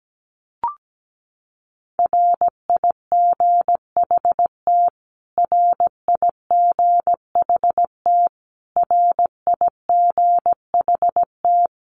Morse code, heard as a keyed tone, about 17 wpm, 700 Hz. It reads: RIGHT RIGHT RIGHT
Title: Top 500 Words Repeated 17wpm